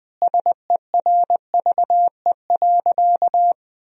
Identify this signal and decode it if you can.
Morse code, heard as a keyed tone, about 21 wpm, 700 Hz. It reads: SERVE.
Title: Sentences from Top 1000 Words - Review Part 4 - 20wpm